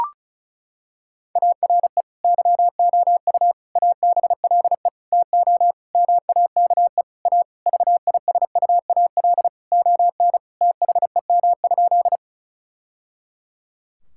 Morse code, heard as a keyed tone, about 35 wpm, 700 Hz. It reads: ARE YOU ABLE TO MAKE A VISUAL ON THEM?
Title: Sentences from Top 1000 Words - Review Part 6 - 35wpm